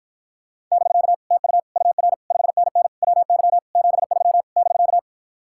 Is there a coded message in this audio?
Yes: Morse code from 0.7 to 5.0 s, heard as a keyed tone, about 52 wpm, 700 Hz. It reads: <BK> TU UR 5NN WY 73 <BK>